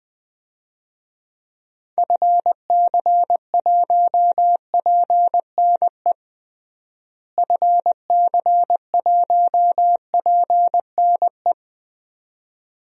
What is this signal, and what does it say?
Morse code, heard as a keyed tone, about 21 wpm, 700 Hz. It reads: FC1PNE FC1PNE
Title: DXCC-100 Call Signs - 6 Characters - Repeated 2x - 20wpm